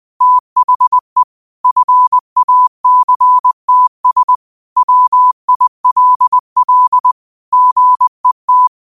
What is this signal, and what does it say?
Morse code, heard as a keyed tone, about 20 wpm, 1000 Hz. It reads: THE FACTS WILL GET
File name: qotdfiles / 20wpm